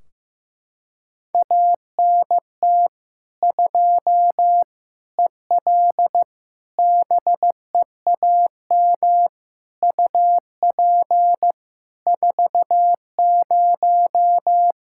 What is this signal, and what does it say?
Morse code, heard as a keyed tone, about 15 wpm, 700 Hz. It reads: ANT 2 EL BEAM UP 40
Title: K1USN SST QSO Practice 15wpm